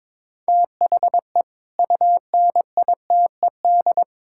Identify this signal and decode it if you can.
Morse code, heard as a keyed tone, about 22 wpm, 700 Hz. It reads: THE UNITED
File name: practice / 22wpm / Build A Phrase - 5-Word Phrases - 22wpm